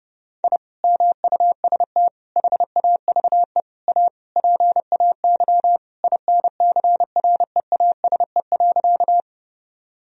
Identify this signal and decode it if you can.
Morse code, heard as a keyed tone, about 32 wpm, 700 Hz. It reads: I MUST HAVE A PAY INCREASE.
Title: Sentences from Top 1000 Words - Review Part 4 - 30wpm